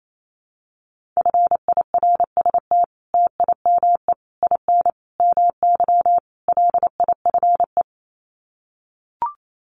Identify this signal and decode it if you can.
Morse code, heard as a keyed tone, about 28 wpm, 700 Hz. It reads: FIRST TIME IN MY LIFE
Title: Sets of 5 Words Encore2 28wpm